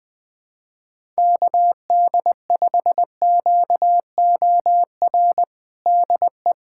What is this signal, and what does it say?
Morse code, heard as a keyed tone, about 20 wpm, 700 Hz. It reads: KD5QOR DE
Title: USA Callsign DE Callsign - 6 Characters - 20wpm